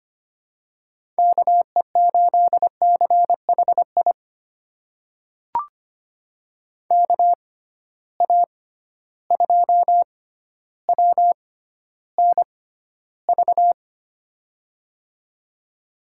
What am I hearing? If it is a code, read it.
Morse code, heard as a keyed tone, about 25 wpm, spaced as for 13 wpm, 700 Hz. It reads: KE8CHI KA2WNV
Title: USA Call Signs - 6 Characters - 6x Character Spacing - 25wpm